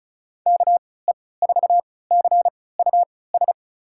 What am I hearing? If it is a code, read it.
Morse code, heard as a keyed tone, about 36 wpm, spaced as for 20 wpm, 700 Hz. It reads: KE4CUS
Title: USA Call Signs - 6 Characters - 3x Character Spacing - 35wpm